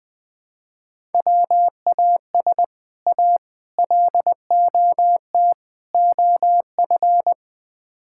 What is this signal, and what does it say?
Morse code, heard as a keyed tone, about 21 wpm, 700 Hz. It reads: WAS A LOT OF
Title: Build A Phrase - 4-Word Phrases - 20wpm